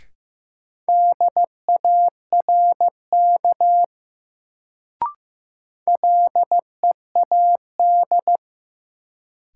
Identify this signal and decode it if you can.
Morse code, heard as a keyed tone, about 15 wpm, 700 Hz. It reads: DARK LEAD
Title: Sentences from Top 500 Words - New Word Drill - 15wpm